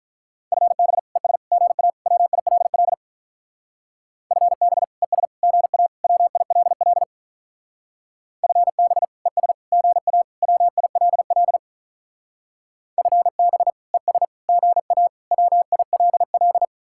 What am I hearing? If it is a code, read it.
Morse code, heard as a keyed tone, about 44 wpm, 700 Hz. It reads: FB ES GA WILL FB ES GA WILL FB ES GA WILL FB ES GA WILL